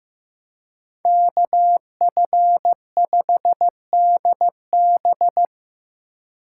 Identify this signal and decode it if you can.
Morse code, heard as a keyed tone, about 18 wpm, 700 Hz. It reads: KF5DB
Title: Speed Racing - USA 2x2 Call Signs 15wpm